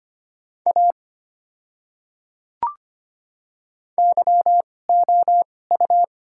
Morse code, heard as a keyed tone, about 25 wpm, spaced as for 18 wpm, 700 Hz. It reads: A YOU
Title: 2x Character Spacing - Top 100 Words - 25wpm